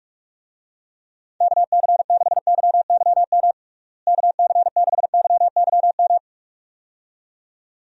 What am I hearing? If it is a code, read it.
Morse code, heard as a keyed tone, about 41 wpm, 700 Hz. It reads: KC6YYM KC6YYM
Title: USA Call Signs - 6 Characters - Repeated 2x - 45wpm